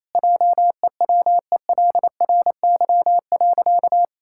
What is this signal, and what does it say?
Morse code, heard as a keyed tone, about 28 wpm, 700 Hz. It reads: JEWELRY.
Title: Pangram Challenge v2 - 28wpm